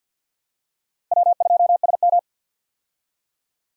Morse code, heard as a keyed tone, about 52 wpm, 700 Hz. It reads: W2SM
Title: USA Call Signs - 4 Characters - Repeated 3x - 50wpm